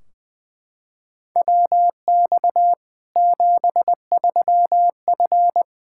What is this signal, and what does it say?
Morse code, heard as a keyed tone, about 20 wpm, 700 Hz. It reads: WX 73F
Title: Practice QSO - Exchange 2 - Info - 20wpm